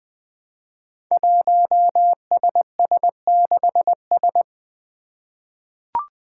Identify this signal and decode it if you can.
Morse code, heard as a keyed tone, about 20 wpm, 700 Hz. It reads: 1SS6S
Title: Five Challenging Letter-Number Combinations 20wpm